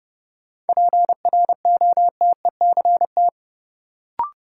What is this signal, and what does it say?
Morse code, heard as a keyed tone, about 30 wpm, 700 Hz. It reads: PROTECT
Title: Sentences from Top 900 Words - New Word Drill - 30wpm